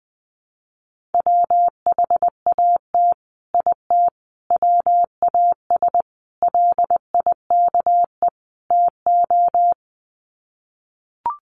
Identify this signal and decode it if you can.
Morse code, heard as a keyed tone, about 20 wpm, 700 Hz. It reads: WHAT IT WAS LIKE TO
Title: Sets of 5 Words Encore 20wpm